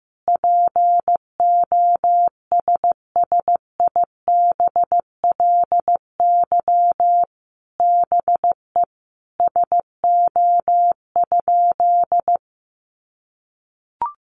Morse code, heard as a keyed tone, about 15 wpm, 700 Hz. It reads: POSSIBLY BE SO?
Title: Sentences from Top 1300 Words 15wpm